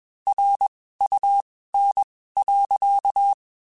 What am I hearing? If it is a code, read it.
Morse code, heard as a keyed tone, about 21 wpm, spaced as for 15 wpm, 800 Hz. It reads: RUN.
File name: (spaced) / QSOs Spaced-mp3s / kc7sac @20 sp